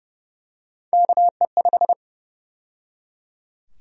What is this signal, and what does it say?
Morse code, heard as a keyed tone, about 31 wpm, 700 Hz. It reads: KE5
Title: USA Call Sign Prefixes - 30wpm